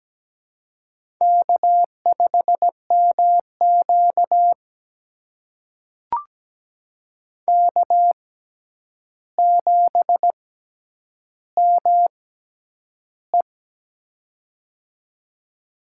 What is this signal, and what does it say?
Morse code, heard as a keyed tone, about 17 wpm, spaced as for 9 wpm, 700 Hz. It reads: K5MQ K7ME
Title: USA Call Signs - 4 Characters - 6x Character Spacing - 17wpm